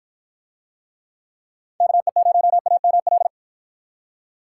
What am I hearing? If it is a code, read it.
Morse code, heard as a keyed tone, about 53 wpm, 700 Hz. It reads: KE0AML